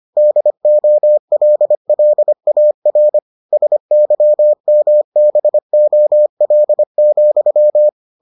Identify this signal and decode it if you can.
Morse code, heard as a keyed tone, about 25 wpm, 600 Hz. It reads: DOLLAR SYMBOL,